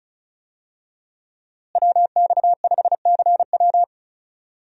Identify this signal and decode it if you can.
Morse code, heard as a keyed tone, about 35 wpm, 700 Hz. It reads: WX5CW
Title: USA Call Signs - 5 Characters - Repeated 2x - 35wpm